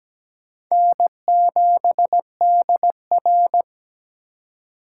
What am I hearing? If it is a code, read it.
Morse code, heard as a keyed tone, about 17 wpm, 700 Hz. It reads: N7DR
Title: K1USN SST QSO Practice - 17wpm